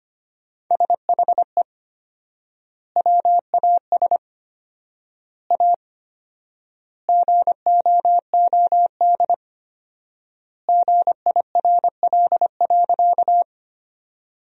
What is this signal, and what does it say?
Morse code, heard as a keyed tone, about 25 wpm, 700 Hz. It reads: SHE WAS A GOOD GIRL.
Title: 4x Word-Spacing - Sentences from Top 400 Words - 25wpm